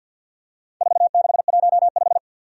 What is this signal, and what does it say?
Morse code, heard as a keyed tone, about 47 wpm, 700 Hz. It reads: 4615
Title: Four Challenging Letter-Number Combinations 50wpm